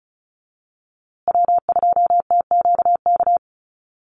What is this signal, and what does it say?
Morse code, heard as a keyed tone, about 34 wpm, 700 Hz. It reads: W2TQK
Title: USA Call Signs Repeated 3x 35wpm